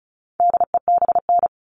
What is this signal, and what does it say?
Morse code, heard as a keyed tone, about 34 wpm, 700 Hz. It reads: DEBN
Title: All Cut Numbers - 4-Digit 35wpm